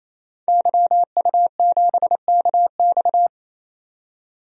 Morse code, heard as a keyed tone, about 28 wpm, 700 Hz. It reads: YU7KX
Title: DXCC-100 Call Signs - 5 Characters - 28wpm